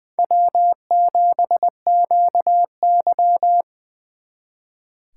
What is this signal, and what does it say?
Morse code, heard as a keyed tone, about 23 wpm, 700 Hz. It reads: W7QY
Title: Speed Racing - USA 1x2 Call Signs 20wpm